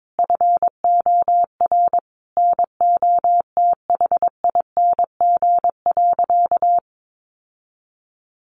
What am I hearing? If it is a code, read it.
Morse code, heard as a keyed tone, about 22 wpm, 700 Hz. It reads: FOR NOTHING.
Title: Sentences from Top 1400 Words 22wpm